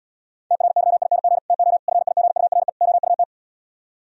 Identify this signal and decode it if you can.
Morse code, heard as a keyed tone, about 77 wpm, 700 Hz. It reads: TRYING TO FIGURE OUT